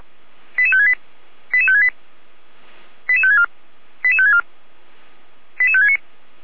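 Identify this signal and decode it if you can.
Five-tone selective call (ZVEI), heard as quick runs of five tones, tones of about 80 ms. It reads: 88568 88568 88564 88564 88569
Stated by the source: ZVEI five-tone call 88102 received by SDR, recorded on the Feuerwehr Orlamünde (fire brigade) audio server.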